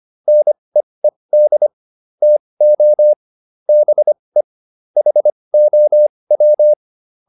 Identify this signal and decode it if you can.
Morse code, heard as a keyed tone, about 25 wpm, spaced as for 20 wpm, 600 Hz. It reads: NEED TO BE HOW
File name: WD204a 20